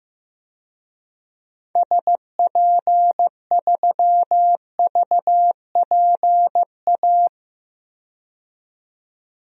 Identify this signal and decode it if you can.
Morse code, heard as a keyed tone, about 16 wpm, 700 Hz. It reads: SP3VPA
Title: DXCC-100 Call Signs - 6 Characters - 15wpm